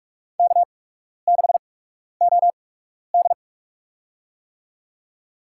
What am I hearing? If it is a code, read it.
Morse code, heard as a keyed tone, about 44 wpm, spaced as for 12 wpm, 690 Hz. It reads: K6OD
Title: USA Call Signs - 4 Characters - 8x Character Spacing - 45wpm